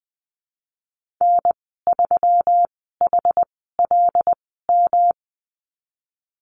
Morse code, heard as a keyed tone, about 20 wpm, spaced as for 15 wpm, 700 Hz. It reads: N3HLM
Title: 128 Lesson 21g - Number 6 - Call Signs 20wpm